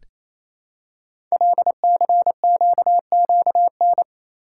Morse code, heard as a keyed tone, about 27 wpm, 700 Hz. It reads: LCQQN
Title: Code Groups - 5 Letters - 28wpm